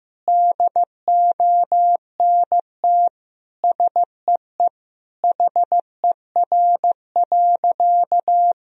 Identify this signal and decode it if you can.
Morse code, heard as a keyed tone, about 15 wpm, 700 Hz. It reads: DONT SEE HER.
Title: Sentences from Top 1000 Words - Review Part 1 - 15wpm